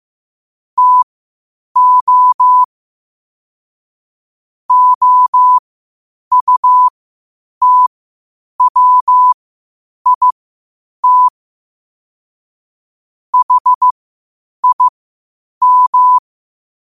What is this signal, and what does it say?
Morse code, heard as a keyed tone, about 15 wpm, spaced as for 8 wpm, 1000 Hz. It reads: TO OUTWIT HIM
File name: qotdfiles / 8wpm